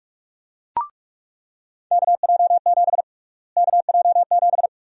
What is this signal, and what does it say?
Morse code, heard as a keyed tone, about 42 wpm, 700 Hz. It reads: KJ7 KJ7
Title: USA Call Sign Prefixes - Repeated 2x - 45wpm